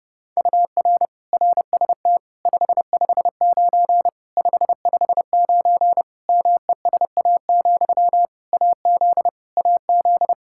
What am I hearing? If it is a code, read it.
Morse code, heard as a keyed tone, about 30 wpm, 700 Hz. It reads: UR RST 559 559 MESA, AZ AZ
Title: Practice QSO - Exchange 1 - Beginning - 30wpm